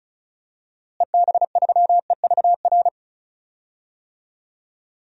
Morse code, heard as a keyed tone, about 36 wpm, 700 Hz. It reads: EB3EVR
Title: DXCC-100 Call Signs - 6 Characters - Repeated 3x - 35wpm